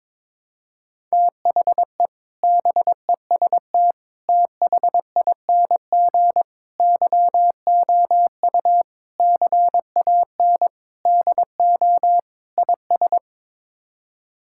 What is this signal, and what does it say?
Morse code, heard as a keyed tone, about 22 wpm, 700 Hz. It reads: THE BEST THING YOU CAN DO IS
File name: practice / 22wpm / Sets of 7 Words - 22wpm